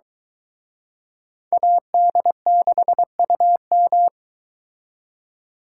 Morse code, heard as a keyed tone, about 23 wpm, 700 Hz. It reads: AD6UM